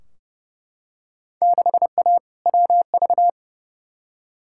Morse code, heard as a keyed tone, about 29 wpm, 700 Hz. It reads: 6A WV
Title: Field Day Exchanges - 30wpm